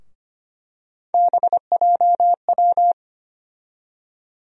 Morse code, heard as a keyed tone, about 25 wpm, 700 Hz. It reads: BJW
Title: USA Call Sign Postfixes - 3 Letters 25wpm